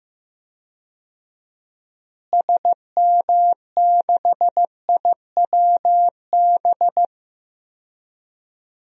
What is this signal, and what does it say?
Morse code, heard as a keyed tone, about 15 wpm, 700 Hz. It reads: SM6IWB
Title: DXCC-100 Call Signs - 6 Characters - Repeated 2x - 15wpm